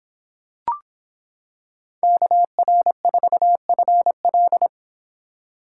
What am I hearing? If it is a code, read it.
Morse code, heard as a keyed tone, about 26 wpm, 700 Hz. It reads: KR4FL